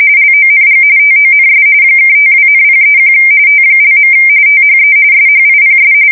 RTTY, about 45.45 baud. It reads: LONG TONS (52,031 T). THE SHIPS
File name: fsk170 / USS-Missouri